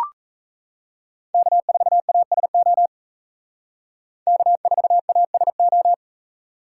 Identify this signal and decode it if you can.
Morse code, heard as a keyed tone, about 35 wpm, 700 Hz. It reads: K4ASO K4ASO
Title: Speed Racing - USA 1x3 Call Signs 28wpm